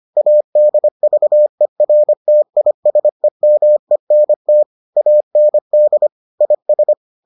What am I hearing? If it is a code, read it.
Morse code, heard as a keyed tone, about 25 wpm, 600 Hz. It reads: ADVERTISEMENT AND IS